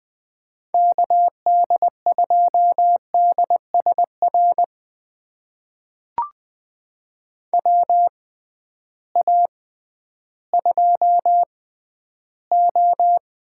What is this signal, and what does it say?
Morse code, heard as a keyed tone, about 20 wpm, spaced as for 12 wpm, 700 Hz. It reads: KD2DSR WA2O
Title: USA Call Signs - 6 Characters - 6x Character Spacing - 20wpm